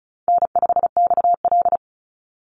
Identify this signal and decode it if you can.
Morse code, heard as a keyed tone, about 34 wpm, 700 Hz. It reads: N5XL
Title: NA SOTA Chaser Callsigns 35wpm